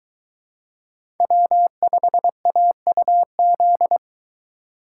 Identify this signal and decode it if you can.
Morse code, heard as a keyed tone, about 22 wpm, 700 Hz. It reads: W5AUZ